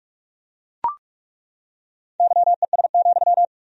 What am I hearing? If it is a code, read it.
Morse code, heard as a keyed tone, about 41 wpm, 700 Hz. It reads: YES,
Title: 4x Word-Spacing - Sentences from Top 500 Words - 45wpm